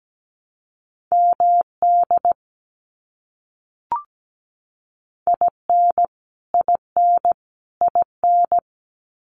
Morse code, heard as a keyed tone, about 17 wpm, 700 Hz. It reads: MD IN IN IN
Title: US States Repeated 3x 17wpm